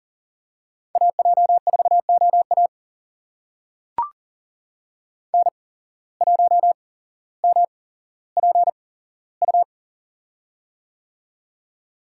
Morse code, heard as a keyed tone, about 39 wpm, spaced as for 17 wpm, 700 Hz. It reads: AJ4OA N1MPU
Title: USA Call Signs - 5 Characters - 8x Character Spacing - 40wpm